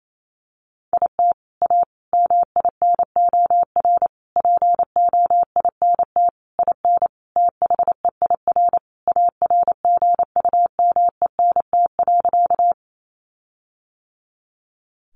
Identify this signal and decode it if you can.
Morse code, heard as a keyed tone, about 28 wpm, 700 Hz. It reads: IT A MINOR POINT IN THEIR ARGUMENT.
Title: Sentences from Top 1500 Words 28wpm